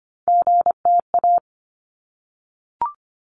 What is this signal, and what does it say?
Morse code, heard as a keyed tone, about 25 wpm, 700 Hz. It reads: GTA
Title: ARRL Sections 25wpm